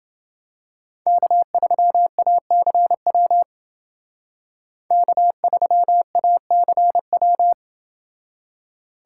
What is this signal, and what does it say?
Morse code, heard as a keyed tone, about 28 wpm, 700 Hz. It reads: K3ACW K3ACW